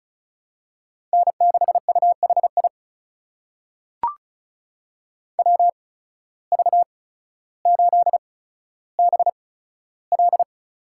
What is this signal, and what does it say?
Morse code, heard as a keyed tone, about 36 wpm, spaced as for 15 wpm, 700 Hz. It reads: N6UHI WV8BL
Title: USA Call Signs - 5 Characters - 8x Character Spacing - 35wpm